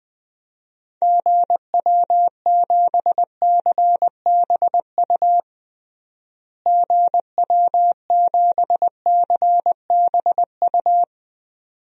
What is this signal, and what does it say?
Morse code, heard as a keyed tone, about 21 wpm, 700 Hz. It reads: GW7CBU GW7CBU
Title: DXCC-100 Call Signs - 6 Characters - Repeated 2x - 20wpm